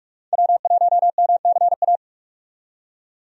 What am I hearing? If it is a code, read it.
Morse code, heard as a keyed tone, about 43 wpm, 700 Hz. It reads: W1MCA
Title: USA Call Signs - 5 Characters - 45wpm